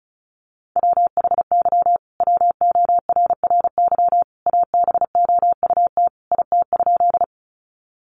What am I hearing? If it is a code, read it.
Morse code, heard as a keyed tone, about 34 wpm, 700 Hz. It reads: WHY WORRY ABOUT IT?
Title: Sentences from Top 1100 Words 35wpm